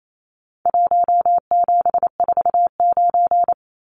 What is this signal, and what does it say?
Morse code, heard as a keyed tone, about 28 wpm, 700 Hz. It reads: 1749
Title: Numbers - Contest Style - 4-Digit 28wpm